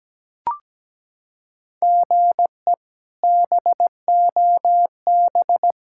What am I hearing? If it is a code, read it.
Morse code, heard as a keyed tone, about 17 wpm, 700 Hz. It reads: GE BOB
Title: K1USN SST QSO Practice - 17wpm